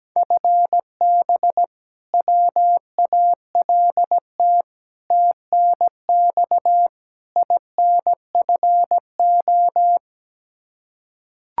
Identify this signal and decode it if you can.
Morse code, heard as a keyed tone, about 17 wpm, 700 Hz. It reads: FB WALT TNX INFO
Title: Practice QSO - Exchange 3 - More Info - 17wpm